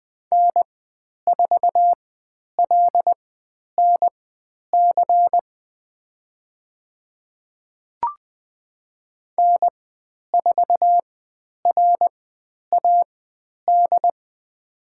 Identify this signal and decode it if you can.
Morse code, heard as a keyed tone, about 20 wpm, spaced as for 10 wpm, 700 Hz. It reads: N4LNC N4RAD
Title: 109 Lesson 18g - Letter m - Call Signs 20wpm